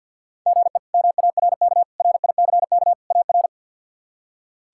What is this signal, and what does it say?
Morse code, heard as a keyed tone, about 52 wpm, 700 Hz. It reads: GE MARK RICK AR
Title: K1USN SST QSO Practice - 50wpm